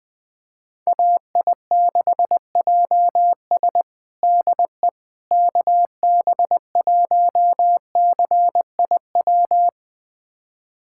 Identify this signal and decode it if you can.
Morse code, heard as a keyed tone, about 20 wpm, 700 Hz. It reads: AI6JS DE KB1CIW
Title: USA Callsign DE Callsign 20wpm